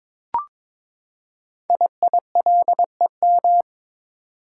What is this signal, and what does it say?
Morse code, heard as a keyed tone, about 22 wpm, 700 Hz. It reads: IILEM
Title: Code Groups - 5 Letter:Numbers - 22wpm